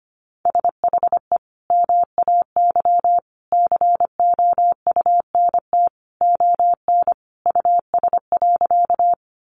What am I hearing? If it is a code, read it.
Morse code, heard as a keyed tone, about 25 wpm, 700 Hz. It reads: SHE MAY COUNT ON US.
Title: Sentences from Top 1500 Words 25wpm